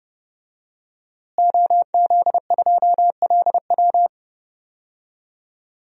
Morse code, heard as a keyed tone, about 30 wpm, 700 Hz. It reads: OZ2LW
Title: DXCC-100 Call Signs - 5 Characters - Repeated 3x - 30wpm